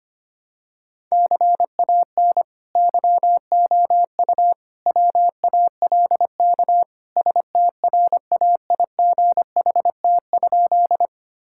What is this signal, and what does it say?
Morse code, heard as a keyed tone, about 25 wpm, 700 Hz. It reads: CAN YOU WALK STRAIGHT?
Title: Sentences from Top 1000 Words - Review Part 6 - 25wpm